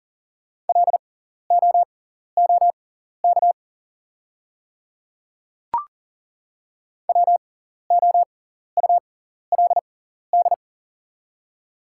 Morse code, heard as a keyed tone, about 38 wpm, spaced as for 14 wpm, 700 Hz. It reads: LOOK WOULD
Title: 6x Character Spacing - Top 300 Words - 40wpm